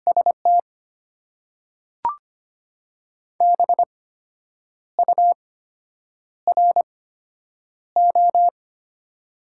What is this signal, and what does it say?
Morse code, heard as a keyed tone, about 25 wpm, spaced as for 8 wpm, 700 Hz. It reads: ST BURO
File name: Top 90 QSO Elements - 8x Character Spacing - 25wpm